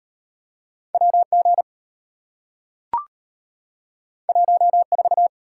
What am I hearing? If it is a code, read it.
Morse code, heard as a keyed tone, about 33 wpm, 690 Hz. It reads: WG 14